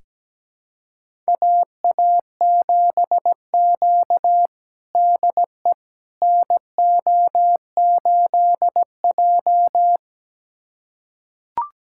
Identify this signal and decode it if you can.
Morse code, heard as a keyed tone, about 17 wpm, 700 Hz. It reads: AA7Q DE NO8J
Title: USA Callsign DE Callsign - 4 Characters - 17wpm